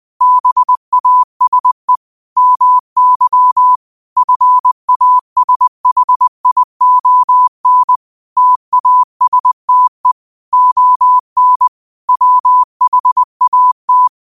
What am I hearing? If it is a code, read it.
Morse code, heard as a keyed tone, about 20 wpm, 1000 Hz. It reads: BASE MY FASHION TASTE ON WHAT
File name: qotdfiles / 20wpm